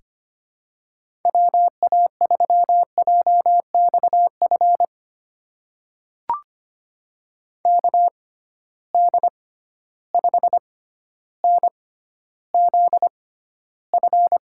Morse code, heard as a keyed tone, about 25 wpm, spaced as for 13 wpm, 700 Hz. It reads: WA3JXF KD5NZF
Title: USA Call Signs - 6 Characters - 6x Character Spacing - 25wpm